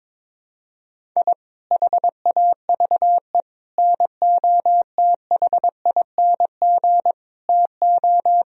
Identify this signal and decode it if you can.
Morse code, heard as a keyed tone, about 22 wpm, 700 Hz. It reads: I HAVE NOTHING TO
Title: Sentences from Top 1000 Words - Review Part 1 - 22wpm